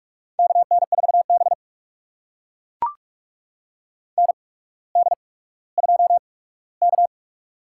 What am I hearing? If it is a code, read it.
Morse code, heard as a keyed tone, about 44 wpm, spaced as for 19 wpm, 700 Hz. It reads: KN4B ND2K
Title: USA Call Signs - 4 Characters - 8x Character Spacing - 45wpm